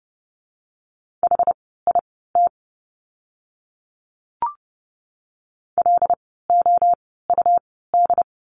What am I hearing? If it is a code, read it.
Morse code, heard as a keyed tone, about 30 wpm, spaced as for 17 wpm, 700 Hz. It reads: HIT LOUD
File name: Lesson 17f - Letter u - Words 30wpm